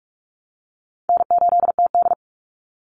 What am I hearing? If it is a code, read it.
Morse code, heard as a keyed tone, about 44 wpm, 700 Hz. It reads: N8TD
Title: Easy Call Signs Repeated 3x - 45wpm